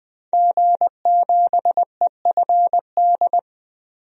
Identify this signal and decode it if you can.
Morse code, heard as a keyed tone, about 20 wpm, 700 Hz. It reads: G7EFD
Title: DXCC-100 Call Signs - 5 Characters - 20wpm